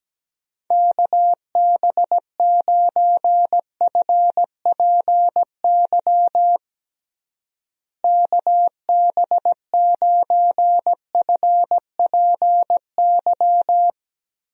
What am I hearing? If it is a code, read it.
Morse code, heard as a keyed tone, about 17 wpm, 700 Hz. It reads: KB9FPY KB9FPY
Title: USA Call Signs - 6 Characters - Repeated 3x - 17wpm